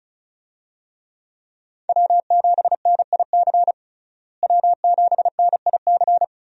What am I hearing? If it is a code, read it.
Morse code, heard as a keyed tone, about 36 wpm, 700 Hz. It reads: W7NIC W7NIC
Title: USA Call Signs - 5 Characters - Repeated 3x - 35wpm